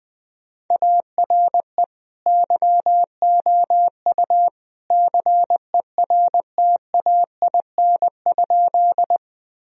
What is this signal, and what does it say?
Morse code, heard as a keyed tone, about 20 wpm, 700 Hz. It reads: ARE YOU CERTAIN?
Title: Sentences from Top 1000 Words - Review Part 2 - 20wpm